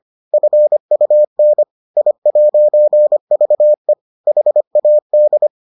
Morse code, heard as a keyed tone, about 25 wpm, 600 Hz. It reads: FUN I'VE HAD